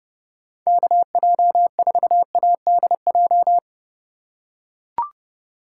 Morse code, heard as a keyed tone, about 29 wpm, 700 Hz. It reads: KJ4ADJ
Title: USA Call Signs - 6 Characters - 4x Character Spacing - 30wpm